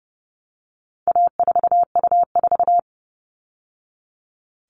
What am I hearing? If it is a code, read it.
Morse code, heard as a keyed tone, about 30 wpm, 700 Hz. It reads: A4U4
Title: All Cut Numbers - 4-Digit 30wpm